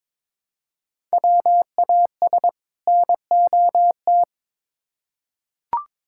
Morse code, heard as a keyed tone, about 22 wpm, 700 Hz. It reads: WAS NOT
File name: Build A Phrase - 2-Word Phrases - 22wpm